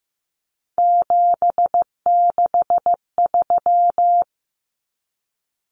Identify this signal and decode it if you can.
Morse code, heard as a keyed tone, about 15 wpm, 700 Hz. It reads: 763
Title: Numbers - 3-Digit 15wpm